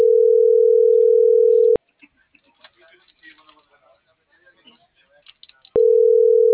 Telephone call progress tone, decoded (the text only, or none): ringback tone